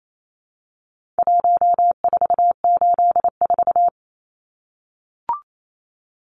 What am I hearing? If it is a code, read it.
Morse code, heard as a keyed tone, about 28 wpm, 700 Hz. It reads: 1484
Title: Numbers - Contest Style - 4-Digit 28wpm